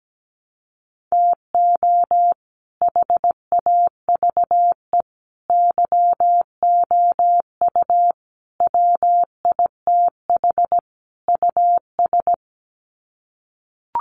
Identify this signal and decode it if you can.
Morse code, heard as a keyed tone, about 17 wpm, 700 Hz. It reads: TO HAVE YOU WITH US
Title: Sets of 5 Words 17wpm